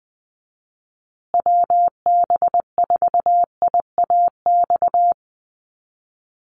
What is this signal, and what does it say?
Morse code, heard as a keyed tone, about 20 wpm, 700 Hz. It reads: WB4IAX
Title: USA Call Signs Repeated 3x 20wpm